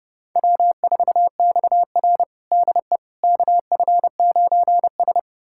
Morse code, heard as a keyed tone, about 32 wpm, 700 Hz. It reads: W4XR DE KF9S
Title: USA Callsign DE Callsign - 4 Characters - 30wpm